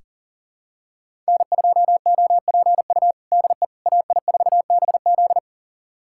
Morse code, heard as a keyed tone, about 37 wpm, 700 Hz. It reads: N2OPU DE AI4BZ
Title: USA Callsign DE Callsign 40wpm